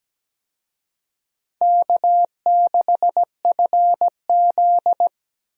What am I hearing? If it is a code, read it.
Morse code, heard as a keyed tone, about 17 wpm, 700 Hz. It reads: K6FZ
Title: USA Call Signs - 4 Characters - Repeated 2x - 17wpm